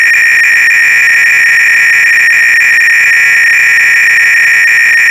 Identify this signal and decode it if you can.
300-baud modem data.